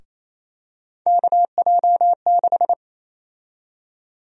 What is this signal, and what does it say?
Morse code, heard as a keyed tone, about 28 wpm, 700 Hz. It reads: KJ6
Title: USA Call Sign Prefixes - Repeated 2x - 28wpm